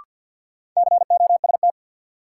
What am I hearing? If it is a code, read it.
Morse code, heard as a keyed tone, about 49 wpm, 700 Hz. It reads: COST
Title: Sentences from Top 800 Words - New Word Drill - 50wpm